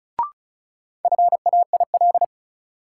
Morse code, heard as a keyed tone, about 34 wpm, 700 Hz. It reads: FAIL
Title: Sentences from Top 1100 Words - New Word Drill - 35wpm